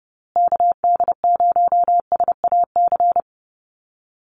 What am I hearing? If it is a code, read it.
Morse code, heard as a keyed tone, about 30 wpm, 700 Hz. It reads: KD0SAC
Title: USA Call Signs Repeated 2x 30wpm